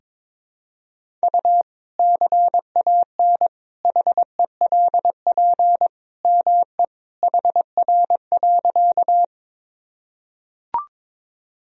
Morse code, heard as a keyed tone, about 22 wpm, 700 Hz. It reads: U CAN HELP ME HR.
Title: Sentences from Top 200 Words and Abbreviations 22wpm